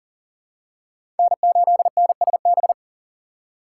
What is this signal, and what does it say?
Morse code, heard as a keyed tone, about 40 wpm, 700 Hz. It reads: N8NSB
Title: USA Call Signs - 5 Characters - 3x Character Spacing - 40wpm